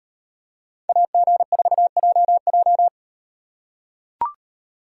Morse code, heard as a keyed tone, about 36 wpm, 700 Hz. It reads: AG4JJ